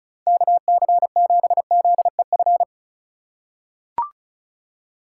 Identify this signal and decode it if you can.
Morse code, heard as a keyed tone, about 36 wpm, 700 Hz. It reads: KC7ZEF
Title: USA Call Signs - 6 Characters - 4x Character Spacing - 35wpm